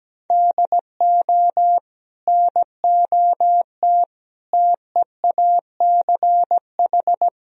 Morse code, heard as a keyed tone, about 17 wpm, 700 Hz. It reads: DO NOT TEACH
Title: Sentences from Top 1000 Words - Review Part 2 - 17wpm